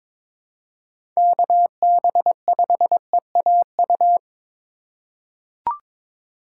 Morse code, heard as a keyed tone, about 22 wpm, 700 Hz. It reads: KB5EAU
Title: USA Call Signs - 3x Character Spacing - 22wpm